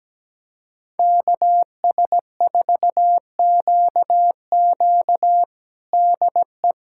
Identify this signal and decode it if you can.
Morse code, heard as a keyed tone, about 17 wpm, 700 Hz. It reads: KS4QQ DE
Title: USA Callsign DE Callsign - 5 Characters - 17wpm